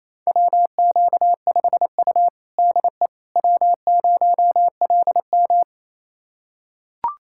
Morse code, heard as a keyed tone, about 28 wpm, 700 Hz. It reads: WQ5U DE W0LM
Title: USA Callsign DE Callsign - 4 Characters - 28wpm